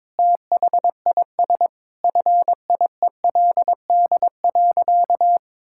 Morse code, heard as a keyed tone, about 22 wpm, 700 Hz. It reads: THIS FIELD.
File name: practice / 22wpm / Sentences with 4-Word Phrases - 22wpm